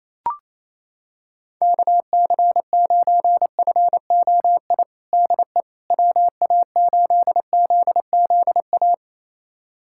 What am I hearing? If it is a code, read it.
Morse code, heard as a keyed tone, about 28 wpm, 700 Hz. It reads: KC9FOI DE WA8ZZA
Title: USA Callsign DE Callsign - 6 Characters - 28wpm